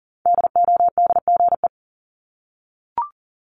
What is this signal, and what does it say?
Morse code, heard as a keyed tone, about 40 wpm, 700 Hz. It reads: DODGE
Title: Multi-letter Phonemes Repeated 3x 40wpm